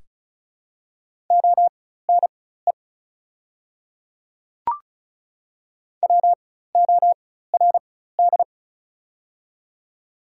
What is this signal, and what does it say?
Morse code, heard as a keyed tone, about 36 wpm, spaced as for 16 wpm, 700 Hz. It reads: ONE WORD
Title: 4x Character Spacing - Top 300 Words - 35wpm